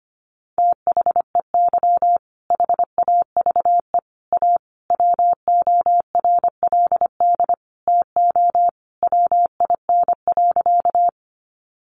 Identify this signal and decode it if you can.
Morse code, heard as a keyed tone, about 25 wpm, 700 Hz. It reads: THEY HAVE A WORLD TO WIN.
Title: Sentences from Top 1500 Words 25wpm